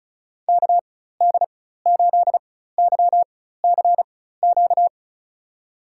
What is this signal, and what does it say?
Morse code, heard as a keyed tone, about 34 wpm, spaced as for 16 wpm, 700 Hz. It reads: KD8YCQ
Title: USA Call Signs - 6 Characters - 4x Character Spacing - 35wpm